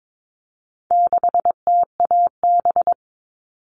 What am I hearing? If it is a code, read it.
Morse code, heard as a keyed tone, about 22 wpm, 700 Hz. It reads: 6TAB
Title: All Cut Numbers - 4-Digit 22wpm